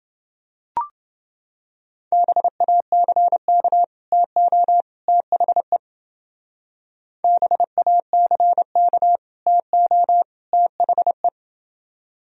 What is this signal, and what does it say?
Morse code, heard as a keyed tone, about 30 wpm, 700 Hz. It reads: BACK TO THE BACK TO THE